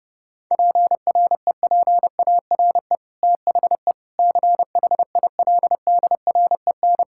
Morse code, heard as a keyed tone, about 32 wpm, 700 Hz. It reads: PREPARE THE CHILDREN
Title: Sentences from Top 1000 Words - Review Part 6 - 30wpm